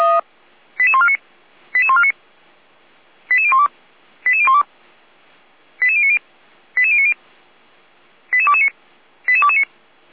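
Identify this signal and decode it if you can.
Five-tone selective call (ZVEI), heard as quick runs of five tones, tones of about 70 ms. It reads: 88149 88149 88012 88012 88090 88090 88229 88229
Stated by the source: ZVEI five-tone call 88501 received by SDR.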